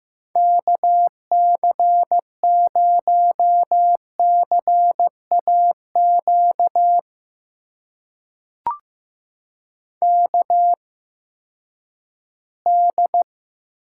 Morse code, heard as a keyed tone, about 15 wpm, spaced as for 10 wpm, 700 Hz. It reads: KC0CAQ KD
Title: USA Call Signs - 6 Characters - 8x Character Spacing - 15wpm